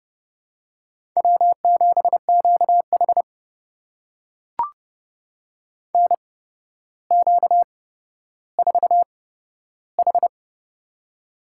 Morse code, heard as a keyed tone, about 30 wpm, spaced as for 13 wpm, 700 Hz. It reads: W7QH NQ4H
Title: USA Call Signs - 8x Character Spacing - 30wpm